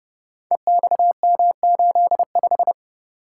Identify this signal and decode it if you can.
Morse code, heard as a keyed tone, about 30 wpm, 700 Hz. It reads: EXM85
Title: Code Groups - 5 Letter:Numbers - 30wpm